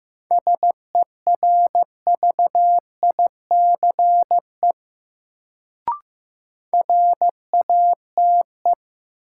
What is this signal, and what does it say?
Morse code, heard as a keyed tone, about 15 wpm, 700 Hz. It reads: SERVICE RATE
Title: Sentences from Top 700 Words - New Word Drill - 15wpm